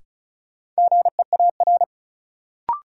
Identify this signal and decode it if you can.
Morse code, heard as a keyed tone, about 36 wpm, 700 Hz. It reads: GEAR
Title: Sentences from Top 1400 Words - New Word Drill - 35wpm